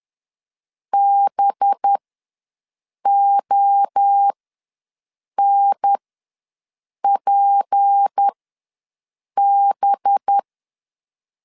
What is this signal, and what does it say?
Morse code, heard as a keyed tone, about 10 wpm, spaced as for 6 wpm, 800 Hz. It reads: BONPB
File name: LEZIONI / 7 GF9Z / 7 GF ws 10wpm 3918